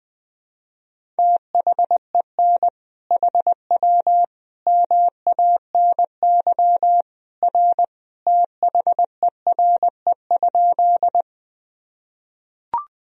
Morse code, heard as a keyed tone, about 20 wpm, 700 Hz. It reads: THEN HW MANY R THERE?
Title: Sentences from Top 100 Words and Abbreviations 20wpm